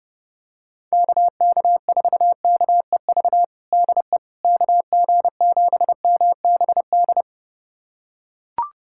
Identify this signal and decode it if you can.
Morse code, heard as a keyed tone, about 30 wpm, 700 Hz. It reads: KK4KEV DE KG7MBD
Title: USA Callsign DE Callsign - 6 Characters - 30wpm